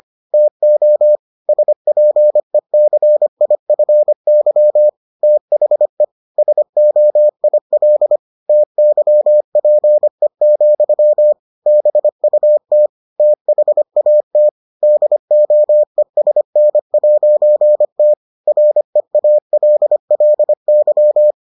Morse code, heard as a keyed tone, about 25 wpm, 600 Hz. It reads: TO SPECIFY THE SOIL TYPE, BUT THAT DOESN'T REALLY